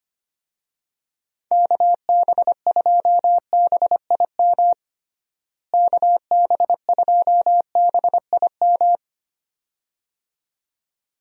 Morse code, heard as a keyed tone, about 25 wpm, 700 Hz. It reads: KB2BIM KB2BIM
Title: USA Call Signs - 6 Characters - Repeated 2x - 25wpm